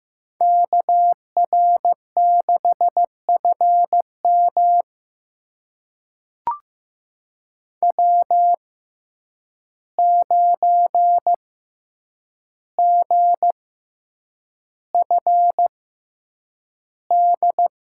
Morse code, heard as a keyed tone, about 15 wpm, spaced as for 8 wpm, 700 Hz. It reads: KR6FM W9GFD
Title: USA Call Signs - 5 Characters - 6x Character Spacing - 15wpm